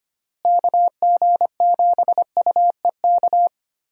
Morse code, heard as a keyed tone, about 25 wpm, 700 Hz. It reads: KG7UEK
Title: USA Call Signs - 6 Characters - Repeated 3x - 25wpm